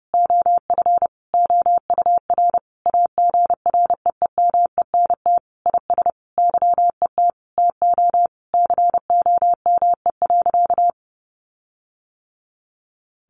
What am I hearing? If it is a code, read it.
Morse code, heard as a keyed tone, about 28 wpm, 700 Hz. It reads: OF OUR AGREEMENT IS YET TO COME.
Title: Sentences from Top 1500 Words 30wpm